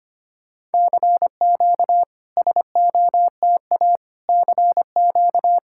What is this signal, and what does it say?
Morse code, heard as a keyed tone, about 25 wpm, 690 Hz. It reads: CQ SOTA CQ
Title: SOTA QSO Practice 25wpm